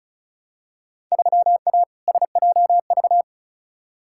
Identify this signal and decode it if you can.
Morse code, heard as a keyed tone, about 36 wpm, 700 Hz. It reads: 3A SJV
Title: Field Day Exchanges Encore - 35wpm